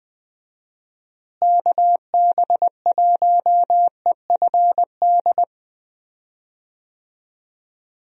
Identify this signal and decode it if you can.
Morse code, heard as a keyed tone, about 20 wpm, 700 Hz. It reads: KB1EFD
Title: USA Call Signs - 6 Characters - Repeated 2x - 20wpm